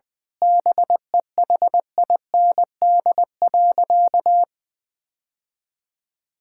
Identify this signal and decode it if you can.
Morse code, heard as a keyed tone, about 20 wpm, 700 Hz. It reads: BEHIND.
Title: Sentences from Top 1600 Words 20wpm